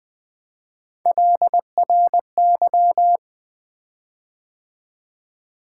Morse code, heard as a keyed tone, about 20 wpm, 700 Hz. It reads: LRY
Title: Three Letter-Number Combinations - 20wpm